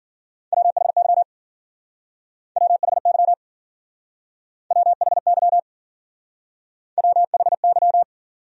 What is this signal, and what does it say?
Morse code, heard as a keyed tone, about 56 wpm, 700 Hz. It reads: WHY WHY WHY WHY